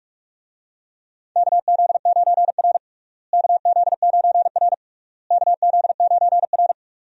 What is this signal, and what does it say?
Morse code, heard as a keyed tone, about 42 wpm, 700 Hz. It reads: KZ9R KZ9R KZ9R
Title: USA Call Signs - 4 Characters - Repeated 3x - 45wpm